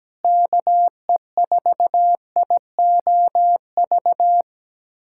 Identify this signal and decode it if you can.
Morse code, heard as a keyed tone, about 17 wpm, 700 Hz. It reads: KE4IOV
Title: USA Callsign DE Callsign - 6 Characters - 17wpm